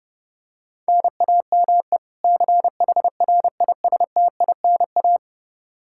Morse code, heard as a keyed tone, about 30 wpm, 700 Hz. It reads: NAME CHRISTINA
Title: Practice QSO - Exchange 1 - Beginning - 30wpm